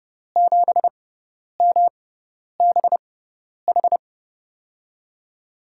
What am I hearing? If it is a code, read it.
Morse code, heard as a keyed tone, about 30 wpm, spaced as for 10 wpm, 700 Hz. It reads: 7MBH
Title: USA Call Signs - 5 Characters - 6x Character Spacing - 30wpm